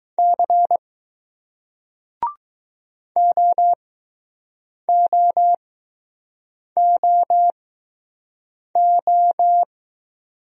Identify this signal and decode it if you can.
Morse code, heard as a keyed tone, about 20 wpm, spaced as for 6 wpm, 700 Hz. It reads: C OOOO